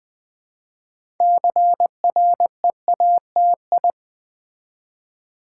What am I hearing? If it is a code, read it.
Morse code, heard as a keyed tone, about 20 wpm, 700 Hz. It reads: CREATI
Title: Build A Word - 3 Letters At A Time - 8-Letter Words - 20wpm